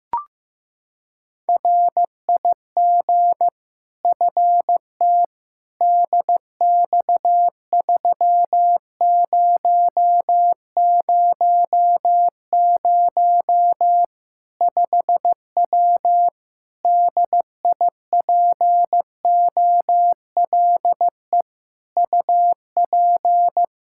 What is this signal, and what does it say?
Morse code, heard as a keyed tone, about 15 wpm, 700 Hz. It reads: RIG FT DX3000 5W DIPOLE UP
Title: Practice QSO - Exchange 2 - Info - 15wpm